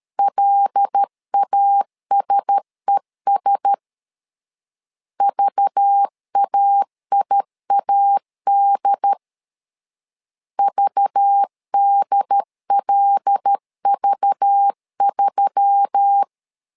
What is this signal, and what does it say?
Morse code, heard as a keyed tone, about 13 wpm, 800 Hz. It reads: LASES VAIAD VDLV3
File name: LEZIONI / 5 LBVC / 5 LBV 12wpm 1855